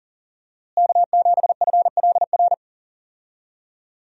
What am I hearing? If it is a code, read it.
Morse code, heard as a keyed tone, about 40 wpm, 700 Hz. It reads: K7FLR
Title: USA Call Signs - 5 Characters - Repeated 3x - 40wpm